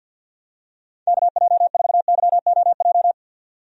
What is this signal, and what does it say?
Morse code, heard as a keyed tone, about 49 wpm, 700 Hz. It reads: KJ4YOJ
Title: USA Call Signs - 6 Characters - 50wpm